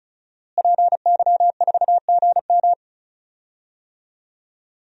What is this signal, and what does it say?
Morse code, heard as a keyed tone, about 34 wpm, 700 Hz. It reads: PY4GM
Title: DXCC-100 Call Signs - 5 Characters - Repeated 3x - 35wpm